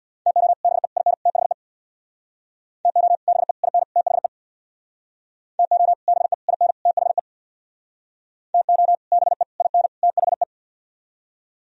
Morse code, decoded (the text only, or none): TO BE IN THE TO BE IN THE TO BE IN THE TO BE IN THE